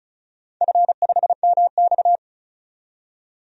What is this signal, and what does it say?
Morse code, heard as a keyed tone, about 34 wpm, 700 Hz. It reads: F5MX
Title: DXCC-100 Call Signs - 4 Characters - 35wpm